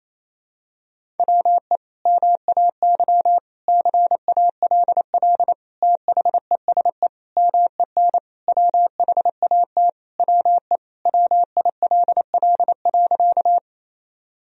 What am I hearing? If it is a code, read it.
Morse code, heard as a keyed tone, about 28 wpm, 700 Hz. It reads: WE MAY CALL THESE MEN WHAT WE WILL.
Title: Sentences from Top 1000 Words - Review Part 1 - 28wpm